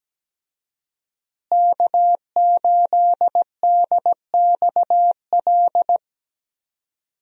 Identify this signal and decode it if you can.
Morse code, heard as a keyed tone, about 17 wpm, 700 Hz. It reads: K8DXL
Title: USA Call Signs - 5 Characters - Repeated 3x - 17wpm